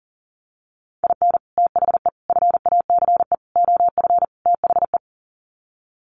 Morse code, heard as a keyed tone, about 40 wpm, 700 Hz. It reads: IN THE FACE OF THE
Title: Sets of 5 Words 40wpm